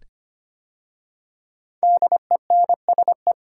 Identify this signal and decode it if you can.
Morse code, heard as a keyed tone, about 25 wpm, 700 Hz. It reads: DENSE
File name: 5 Letter Words - First 3 Letters Given - 25wpm